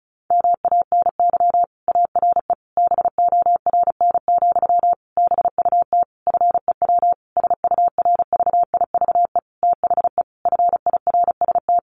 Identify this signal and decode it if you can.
Morse code, heard as a keyed tone, about 34 wpm, 700 Hz. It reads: MANY ARE BORN, BUT FEW SURVIVE THE FIRST
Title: Sentences from Top 1300 Words 35wpm